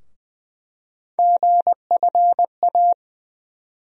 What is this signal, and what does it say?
Morse code, heard as a keyed tone, about 20 wpm, 700 Hz. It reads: GFA
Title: USA Call Sign Postfixes - 3 Letters 20wpm